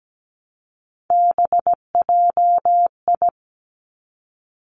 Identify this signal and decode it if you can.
Morse code, heard as a keyed tone, about 17 wpm, 700 Hz. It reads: BJI
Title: Three Challenging Letter-Number Combinations 17wpm